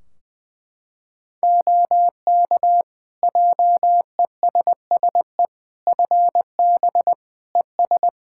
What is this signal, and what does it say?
Morse code, heard as a keyed tone, about 21 wpm, 700 Hz. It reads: OK JESSE FB ES
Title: Practice QSO - Exchange 2 - Info - 20wpm